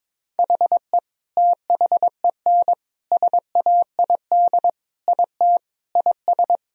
Morse code, heard as a keyed tone, about 22 wpm, 700 Hz. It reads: HE THEN SAID IT IS
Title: Sentences from Top 1000 Words - Review Part 1 - 22wpm